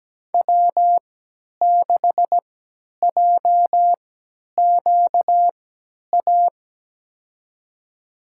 Morse code, heard as a keyed tone, about 17 wpm, spaced as for 10 wpm, 700 Hz. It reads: W6JQA
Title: USA Call Signs - 5 Characters - 3x Character Spacing - 17wpm